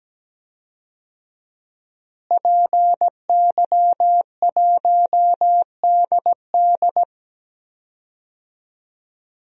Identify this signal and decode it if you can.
Morse code, heard as a keyed tone, about 17 wpm, 700 Hz. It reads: PY1DD
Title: DXCC-100 Call Signs - 5 Characters - Repeated 3x - 17wpm